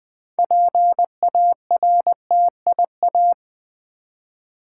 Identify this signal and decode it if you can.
Morse code, heard as a keyed tone, about 20 wpm, 700 Hz. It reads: PARTIA
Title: Build A Word - 3 Letters At A Time - 9-Letter Words - 20wpm